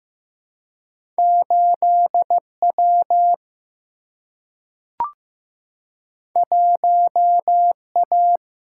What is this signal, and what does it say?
Morse code, heard as a keyed tone, about 15 wpm, 700 Hz. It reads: 8 W 1 A
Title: Single-Letter Single-Number 15wpm